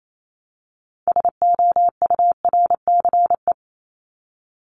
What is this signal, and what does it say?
Morse code, heard as a keyed tone, about 28 wpm, 700 Hz. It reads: SOURCE
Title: ES:S Suffix - Top 100 Words with Base Words 28wpm